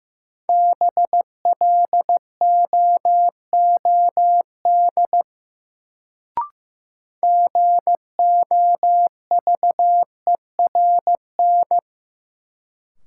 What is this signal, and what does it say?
Morse code, heard as a keyed tone, about 15 wpm, 700 Hz. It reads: BLOOD GOVERN
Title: Sentences from Top 1000 Words - Review Part 2 - New Word Drill - 15wpm